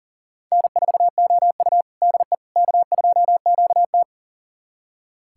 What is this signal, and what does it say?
Morse code, heard as a keyed tone, about 40 wpm, 700 Hz. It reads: N4OU DE K2QT
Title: USA Callsign DE Callsign - 4 Characters - 40wpm